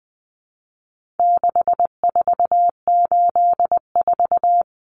Morse code, heard as a keyed tone, about 20 wpm, 700 Hz. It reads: 6484
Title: Numbers - 4-Digit 20wpm